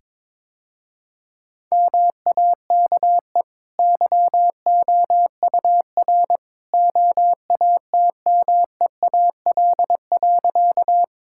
Morse code, heard as a keyed tone, about 22 wpm, 690 Hz. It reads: MAKE YOUR OATMEAL.
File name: practice / 22wpm / Easy Sentences with Hints - 22wpm